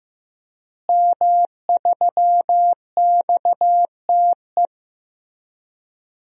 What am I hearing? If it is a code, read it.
Morse code, heard as a keyed tone, about 16 wpm, 700 Hz. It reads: M3XTE
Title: DXCC-100 Call Signs - 5 Characters - Repeated 3x - 15wpm